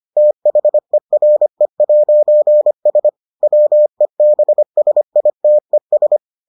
Morse code, heard as a keyed tone, about 25 wpm, 600 Hz. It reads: THERE'S WEBSITES